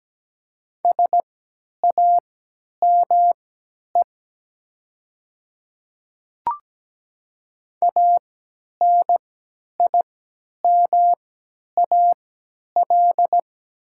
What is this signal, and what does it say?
Morse code, heard as a keyed tone, about 17 wpm, spaced as for 10 wpm, 690 Hz. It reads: SAME ANIMAL
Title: 3x Character Spacing - Top 200 Words - 17wpm